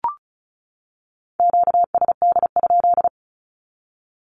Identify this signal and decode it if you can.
Morse code, heard as a keyed tone, about 34 wpm, 700 Hz. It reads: QSD?
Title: Q-Codes 35wpm